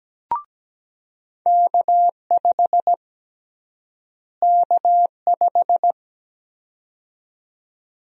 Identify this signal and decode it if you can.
Morse code, heard as a keyed tone, about 17 wpm, 700 Hz. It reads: K5 K5
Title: USA Call Sign Prefixes - Repeated 2x - 17wpm